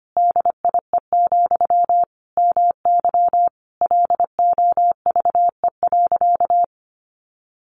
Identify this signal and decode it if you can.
Morse code, heard as a keyed tone, about 25 wpm, 700 Hz. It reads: DIE, MY LOVE.
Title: Sentences from Top 1300 Words 25wpm